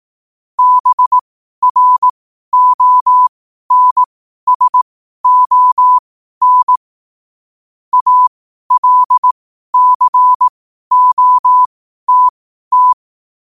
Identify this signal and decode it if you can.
Morse code, heard as a keyed tone, about 18 wpm, spaced as for 12 wpm, 1000 Hz. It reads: BRONSON ALCOTT
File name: qotdfiles / 12wpm